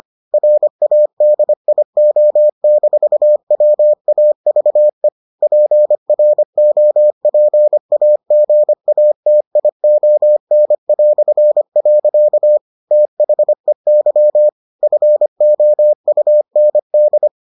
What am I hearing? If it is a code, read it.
Morse code, heard as a keyed tone, about 25 wpm, 600 Hz. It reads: RADIO-WAVE PROPAGATION". THEY FOUND